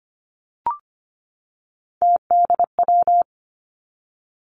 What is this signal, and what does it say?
Morse code, heard as a keyed tone, about 25 wpm, 700 Hz. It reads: TDW
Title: Three Challenging Letter-Number Combinations 25wpm